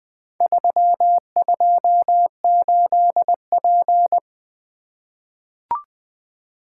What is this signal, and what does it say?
Morse code, heard as a keyed tone, about 20 wpm, 700 Hz. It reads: 328P
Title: Four Letter-Number Combinations - 20wpm